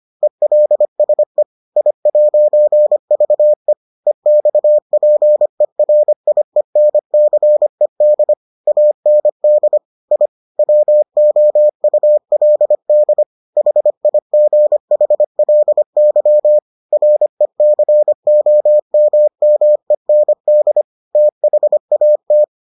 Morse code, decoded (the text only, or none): ELSE I'VE EXPERIENCED AND I WOULD HIGHLY RECOMMEND THAT